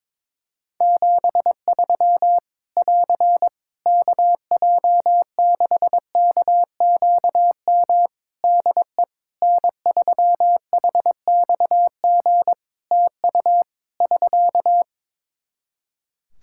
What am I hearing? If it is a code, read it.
Morse code, heard as a keyed tone, about 22 wpm, 700 Hz. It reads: 73 <AR> KJ6KQM DE N3HXG TU <SK>
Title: Practice QSO - Exchange 4 - Ending - 22wpm